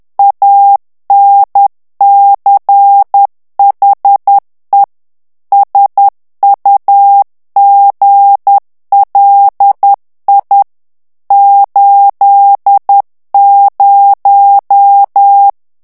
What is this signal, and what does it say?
Morse code, heard as a keyed tone, about 10 wpm, 800 Hz. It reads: ANCHE SUGLI 80
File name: morse mp3 / ITALIEN texte clair 1 vitesse 10 mots minute